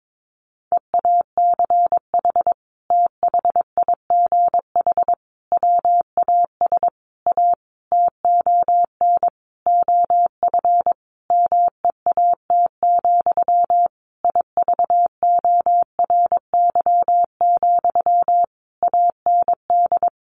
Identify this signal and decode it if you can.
Morse code, heard as a keyed tone, about 22 wpm, 700 Hz. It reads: EACH THIGH WAS A TON OF MEAT, IVORY, AND
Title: A Sound Of Thunder 22wpm